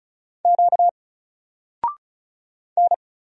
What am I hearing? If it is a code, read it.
Morse code, heard as a keyed tone, about 34 wpm, 700 Hz. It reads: Q N
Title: Single Letter-Number 35wpm